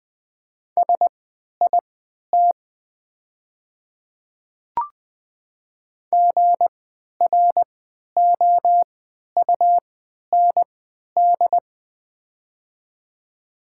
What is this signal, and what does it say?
Morse code, heard as a keyed tone, about 21 wpm, spaced as for 11 wpm, 690 Hz. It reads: SIT GROUND
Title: 3x Character Spacing - Top 500 Words - 20wpm